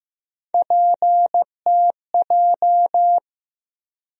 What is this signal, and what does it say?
Morse code, heard as a keyed tone, about 15 wpm, 700 Hz. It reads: PTJ
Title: USA Call Sign Postfixes - 3 Letters 15wpm